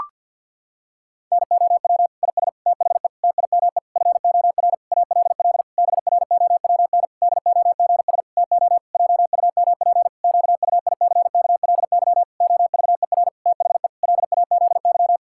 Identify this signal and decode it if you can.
Morse code, heard as a keyed tone, about 52 wpm, 700 Hz. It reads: NOW IS THE TIME FOR ALL BROWN DOGS TO JUMP QUICKLY OVER THE LAZY
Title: Pangram Challenge v2 - 50wpm